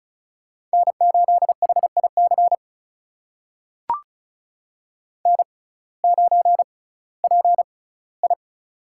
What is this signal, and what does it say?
Morse code, heard as a keyed tone, about 36 wpm, spaced as for 19 wpm, 700 Hz. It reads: N8HIC N9PI
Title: USA Call Signs - 6x Character Spacing - 35wpm